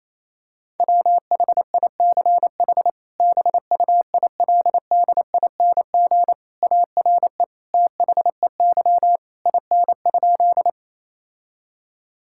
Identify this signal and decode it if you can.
Morse code, heard as a keyed tone, about 28 wpm, 700 Hz. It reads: WHICH BUILDING ARE THEY IN?
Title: Sentences from Top 1000 Words - Review Part 5 - 28wpm